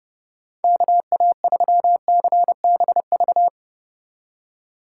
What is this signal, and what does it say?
Morse code, heard as a keyed tone, about 28 wpm, 700 Hz. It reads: KA3CBV